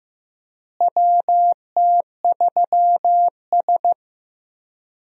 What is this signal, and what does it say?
Morse code, heard as a keyed tone, about 15 wpm, 700 Hz. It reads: WT3S
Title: USA Call Signs - 4 Characters - Repeated 3x - 15wpm